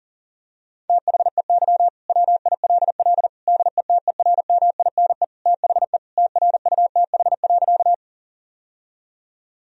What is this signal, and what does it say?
Morse code, heard as a keyed tone, about 40 wpm, 700 Hz. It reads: THEY WILL DETERMINE THE TRUTH.
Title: Sentences from Top 1000 Words - Review Part 4 - 40wpm